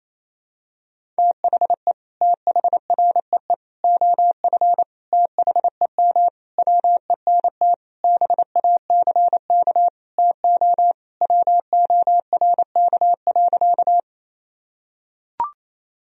Morse code, heard as a keyed tone, about 28 wpm, 700 Hz. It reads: THE THREE OF THEM WENT BACK TO WORK.
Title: Sentences from Top 1000 Words - Review Part 1 - 28wpm